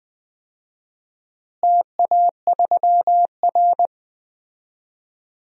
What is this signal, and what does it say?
Morse code, heard as a keyed tone, about 20 wpm, 700 Hz. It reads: TA3R
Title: DXCC-100 Call Signs - 4 Characters - Repeated 2x - 20wpm